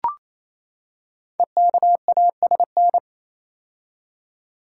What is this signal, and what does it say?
Morse code, heard as a keyed tone, about 28 wpm, 700 Hz. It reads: EKASN
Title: Code Groups - 5 Letters - 28wpm